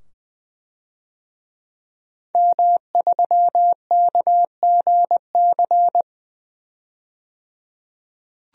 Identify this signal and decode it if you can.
Morse code, heard as a keyed tone, about 21 wpm, 700 Hz. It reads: M3KGC
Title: DXCC-100 Call Signs - 5 Characters - 20wpm